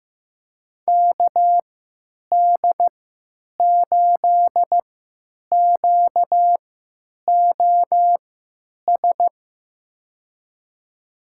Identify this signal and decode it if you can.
Morse code, heard as a keyed tone, about 15 wpm, spaced as for 9 wpm, 700 Hz. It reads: KD8QOS
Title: USA Call Signs - 6 Characters - 3x Character Spacing - 15wpm